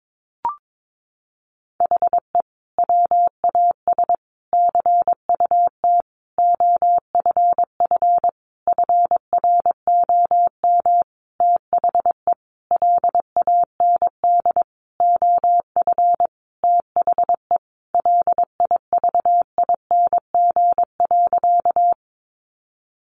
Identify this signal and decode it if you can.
Morse code, heard as a keyed tone, about 22 wpm, 700 Hz. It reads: HE WAS CUT OFF FROM THE LAND OF THE LIVING.
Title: Sentences from Top 1100 Words 22wpm